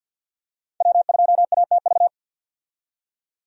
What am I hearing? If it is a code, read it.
Morse code, heard as a keyed tone, about 47 wpm, 700 Hz. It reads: W2ATV
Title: USA Call Signs - 5 Characters - 50wpm